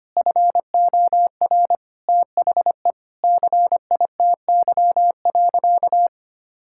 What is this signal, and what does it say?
Morse code, heard as a keyed tone, about 25 wpm, 700 Hz. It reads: FOR THE CITY.
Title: Sentences from Top 1600 Words 25wpm